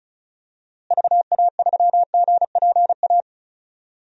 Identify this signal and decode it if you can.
Morse code, heard as a keyed tone, about 36 wpm, 700 Hz. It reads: VA3GPA
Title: DXCC-100 Call Signs - 6 Characters - Repeated 2x - 35wpm